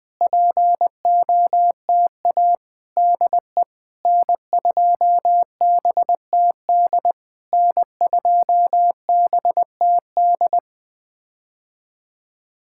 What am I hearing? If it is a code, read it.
Morse code, heard as a keyed tone, about 19 wpm, 700 Hz. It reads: POTA DE N2BTD N2BTD
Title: POTA QSO Practice - 20wpm